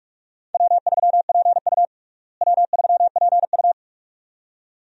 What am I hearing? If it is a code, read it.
Morse code, heard as a keyed tone, about 46 wpm, 700 Hz. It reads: W3PU W3PU
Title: USA Call Signs - 4 Characters - Repeated 2x - 45wpm